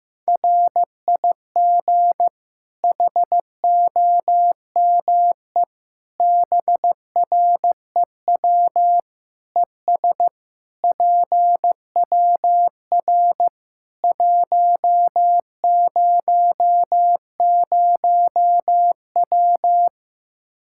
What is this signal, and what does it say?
Morse code, heard as a keyed tone, about 15 wpm, 700 Hz. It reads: RIG HOME BREW ES PWR 100W
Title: Practice QSO - Exchange 2 - Info - 15wpm